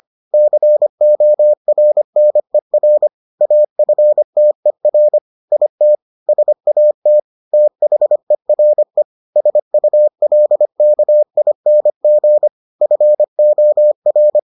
Morse code, heard as a keyed tone, about 25 wpm, 600 Hz. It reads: CORNER AFTER IT SAT THERE SULKING FOR